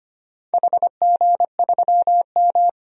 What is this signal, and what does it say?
Morse code, heard as a keyed tone, about 25 wpm, 690 Hz. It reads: HG3M
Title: DXCC-100 Call Signs - 4 Characters - 25wpm